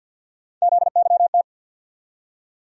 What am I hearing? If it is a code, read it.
Morse code, heard as a keyed tone, about 49 wpm, 700 Hz. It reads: ZYT
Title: USA Call Sign Postfixes - 3 Letters 50wpm